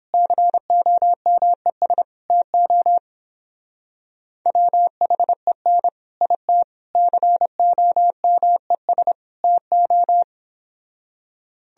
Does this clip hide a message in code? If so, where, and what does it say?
Yes: Morse code from 0.1 to 10.2 s, heard as a keyed tone, about 31 wpm, 700 Hz. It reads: COMES TO WHEN IT COMES TO